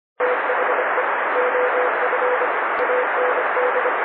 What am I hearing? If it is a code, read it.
Morse code, heard as a keyed tone, about 26 wpm, 500 Hz. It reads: THE OF AND
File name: CW / noise